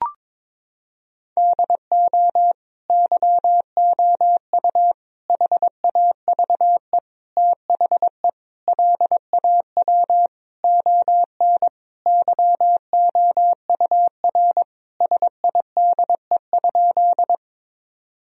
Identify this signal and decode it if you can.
Morse code, heard as a keyed tone, about 22 wpm, 700 Hz. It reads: DO YOU HAVE THE LAW ON YOUR SIDE?
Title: Sentences from Top 1000 Words - Review Part 3 - 22wpm